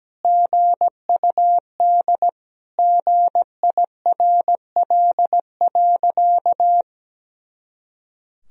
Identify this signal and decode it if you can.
Morse code, heard as a keyed tone, about 17 wpm, 700 Hz. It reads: GUD GIRL.
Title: Sentences from Top 400 Words and Abbreviations 17wpm